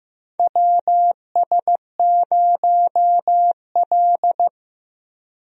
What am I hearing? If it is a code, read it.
Morse code, heard as a keyed tone, about 15 wpm, 700 Hz. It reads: WS0L
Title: USA Call Signs - 4 Characters - Repeated 3x - 15wpm